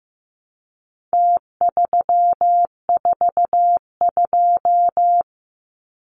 Morse code, heard as a keyed tone, about 15 wpm, 700 Hz. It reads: T342
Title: Numbers - Contest Style - 4-Digit 15wpm